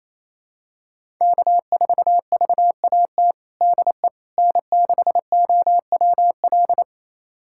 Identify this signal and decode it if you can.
Morse code, heard as a keyed tone, about 28 wpm, 700 Hz. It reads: K4VAT DE N6OWL
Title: USA Callsign DE Callsign - 5 Characters - 28wpm